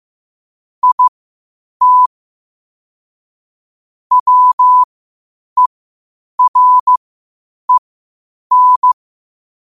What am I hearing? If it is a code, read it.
Morse code, heard as a keyed tone, about 15 wpm, spaced as for 8 wpm, 1000 Hz. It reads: IT WEREN
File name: qotdfiles / 8wpm